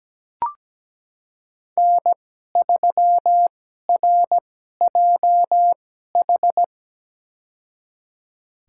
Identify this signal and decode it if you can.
Morse code, heard as a keyed tone, about 17 wpm, spaced as for 12 wpm, 700 Hz. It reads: N3RJH
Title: USA Call Signs - 5 Characters - 2x Character Spacing - 17wpm